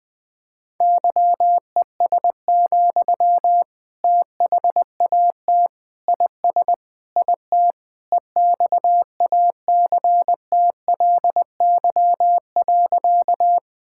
Morse code, heard as a keyed tone, about 20 wpm, 700 Hz. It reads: YES, THAT IS IT EXACTLY.
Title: Sentences from Top 1000 Words - Review Part 6 - 20wpm